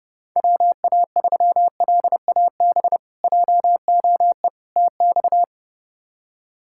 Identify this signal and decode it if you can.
Morse code, heard as a keyed tone, about 30 wpm, 700 Hz. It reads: WA3LAB JOE TX
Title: K1USN SST QSO Practice - 30wpm